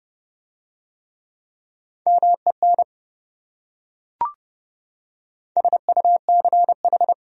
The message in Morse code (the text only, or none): MEN SUCH